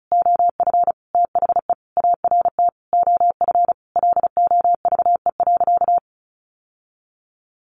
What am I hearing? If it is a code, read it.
Morse code, heard as a keyed tone, about 34 wpm, 700 Hz. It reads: OF THE ART OF LOVE.
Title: Sentences from Top 1500 Words 35wpm